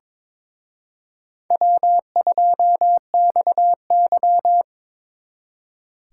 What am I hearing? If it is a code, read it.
Morse code, heard as a keyed tone, about 28 wpm, 700 Hz. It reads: W2XY